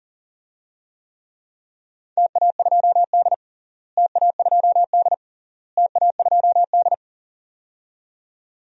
Morse code, heard as a keyed tone, about 40 wpm, 700 Hz. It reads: TA2D TA2D TA2D
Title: DXCC-100 Call Signs - 4 Characters - Repeated 3x - 40wpm